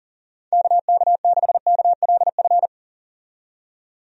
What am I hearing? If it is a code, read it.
Morse code, heard as a keyed tone, about 40 wpm, 700 Hz. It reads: KK6KLF
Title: USA Call Signs - 6 Characters - 40wpm